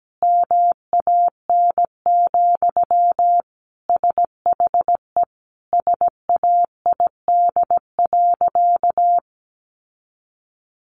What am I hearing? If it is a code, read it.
Morse code, heard as a keyed tone, about 17 wpm, 700 Hz. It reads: MAN, SHE SAID.
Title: Sentences from Top 1100 Words 17wpm